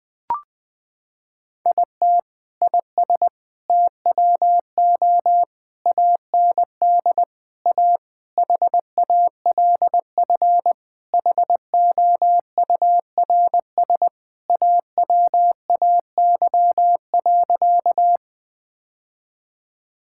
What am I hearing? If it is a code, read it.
Morse code, heard as a keyed tone, about 20 wpm, 700 Hz. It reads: IT IS TWO AND A HALF HOURS AWAY.
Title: Sentences with 4-Word Phrases - 20wpm